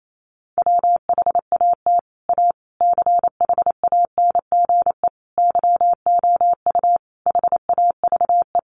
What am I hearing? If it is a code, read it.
Morse code, heard as a keyed tone, about 28 wpm, 700 Hz. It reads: WHAT A CHANGE YOU HAVE
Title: Sentences from Top 600 Words 28wpm